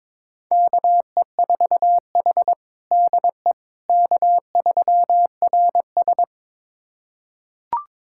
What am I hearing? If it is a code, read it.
Morse code, heard as a keyed tone, about 22 wpm, 700 Hz. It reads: KE4H DE K3RS
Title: USA Callsign DE Callsign - 4 Characters - 22wpm